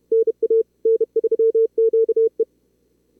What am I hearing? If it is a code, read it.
Morse code, heard as a keyed tone, about 35 wpm, 440 Hz. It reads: NA N3QE
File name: CWSprint / Feb2022 / k5tr-feb-na-sprint-cw-k5ot -2300